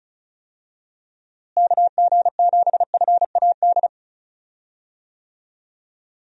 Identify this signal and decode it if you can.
Morse code, heard as a keyed tone, about 36 wpm, 700 Hz. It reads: KG7FAD
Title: USA Call Signs - 6 Characters - Repeated 2x - 35wpm